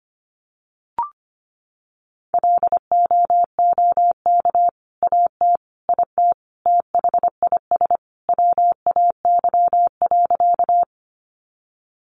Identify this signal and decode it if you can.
Morse code, heard as a keyed tone, about 25 wpm, 700 Hz. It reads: LOOK AT IT THIS WAY.
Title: Sentences from Top 100 Words 25wpm